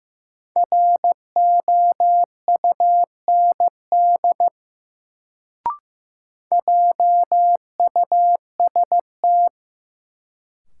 Morse code, heard as a keyed tone, about 15 wpm, 700 Hz. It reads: ROUND JUST
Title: Sentences from Top 200 Words - New Word Drill - 15wpm